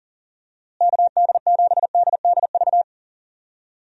Morse code, heard as a keyed tone, about 40 wpm, 700 Hz. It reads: KD7DDV
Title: USA Call Signs - 6 Characters - Repeated 2x - 40wpm